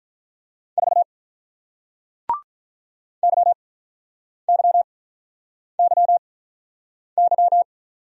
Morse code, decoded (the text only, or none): 4 YYYY